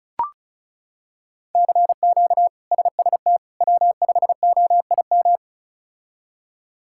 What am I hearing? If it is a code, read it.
Morse code, heard as a keyed tone, about 34 wpm, 700 Hz. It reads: CQ SST W5OIM
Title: K1USN SST QSO Practice - 35wpm